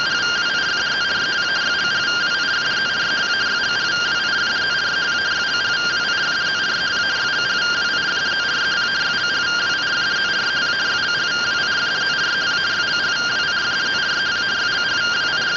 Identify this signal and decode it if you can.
Two-tone teleprinter signal, about 170 Hz shift.